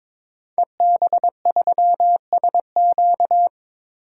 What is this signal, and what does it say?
Morse code, heard as a keyed tone, about 22 wpm, 700 Hz. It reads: EB3SQ
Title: DXCC-100 Call Signs - 5 Characters - 22wpm